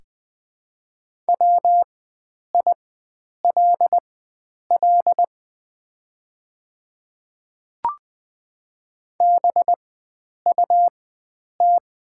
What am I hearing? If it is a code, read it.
Morse code, heard as a keyed tone, about 20 wpm, spaced as for 9 wpm, 690 Hz. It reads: WILL BUT
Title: 4x Character Spacing - Top 100 Words - 20wpm